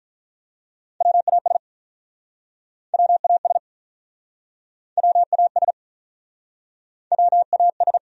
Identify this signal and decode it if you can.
Morse code, heard as a keyed tone, about 47 wpm, 700 Hz. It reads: WAS WAS WAS WAS